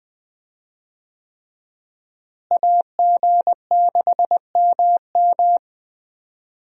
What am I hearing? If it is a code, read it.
Morse code, heard as a keyed tone, about 21 wpm, 700 Hz. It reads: AG6MM
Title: USA Call Signs - 5 Characters - Repeated 3x - 20wpm